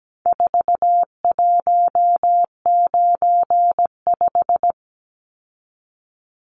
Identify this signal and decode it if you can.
Morse code, heard as a keyed tone, about 17 wpm, 700 Hz. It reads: 4195
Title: Numbers - 4-Digit 17wpm